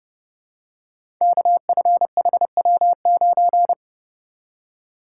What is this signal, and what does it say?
Morse code, heard as a keyed tone, about 29 wpm, 700 Hz. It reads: KFHW9
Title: Code Groups - 5 Letter:Numbers - 30wpm